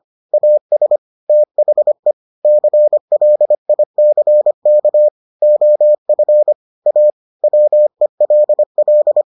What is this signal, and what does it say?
Morse code, heard as a keyed tone, about 25 wpm, 600 Hz. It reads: AS THE CLICK OF A WELL